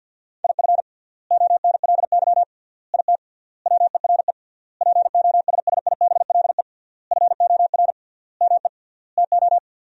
Morse code, heard as a keyed tone, about 52 wpm, 700 Hz. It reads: IF ONLY IT WERE POSSIBLE FOR ME TO
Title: 3x Word-Spacing - Sentences from Top 500 Words - 50wpm